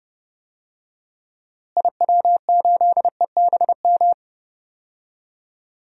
Morse code, heard as a keyed tone, about 30 wpm, 700 Hz. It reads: IW8EBM
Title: DXCC-100 Call Signs - 6 Characters - Repeated 3x - 30wpm